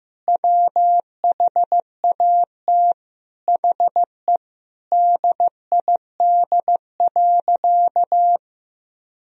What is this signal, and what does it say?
Morse code, heard as a keyed tone, about 16 wpm, 700 Hz. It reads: WHAT HE DID.
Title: Sentences from Top 1000 Words - Review Part 4 - 15wpm